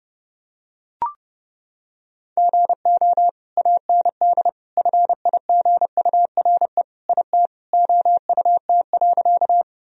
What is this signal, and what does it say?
Morse code, heard as a keyed tone, about 32 wpm, 700 Hz. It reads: GO AND FIGURE IT OUT.
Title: Sentences from Top 1000 Words - Review Part 2 - 30wpm